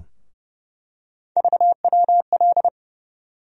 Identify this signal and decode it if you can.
Morse code, heard as a keyed tone, about 30 wpm, 700 Hz. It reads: VWL
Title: USA Call Sign Postfixes - 3 Letters 30wpm